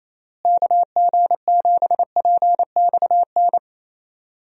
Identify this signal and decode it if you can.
Morse code, heard as a keyed tone, about 27 wpm, 700 Hz. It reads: KG7PXN
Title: USA Call Signs - 6 Characters - Repeated 2x - 28wpm